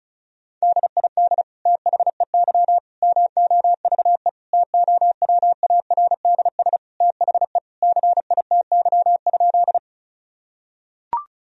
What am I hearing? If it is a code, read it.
Morse code, heard as a keyed tone, about 36 wpm, 700 Hz. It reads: DID THEY MOVE TOWARDS THE CITY?
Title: Sentences from Top 1000 Words - Review Part 2 - 35wpm